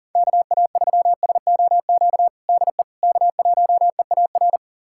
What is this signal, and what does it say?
Morse code, heard as a keyed tone, about 40 wpm, 700 Hz. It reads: KA3SOQ DE K1EAR
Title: USA Callsign DE Callsign 40wpm